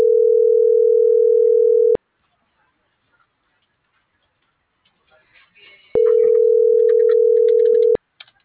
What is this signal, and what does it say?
Ringback tone (North America), 440 + 480 Hz.